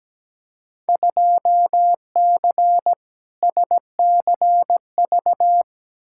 Morse code, heard as a keyed tone, about 17 wpm, 700 Hz. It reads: 2C SCV
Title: Field Day Exchanges Encore - 17wpm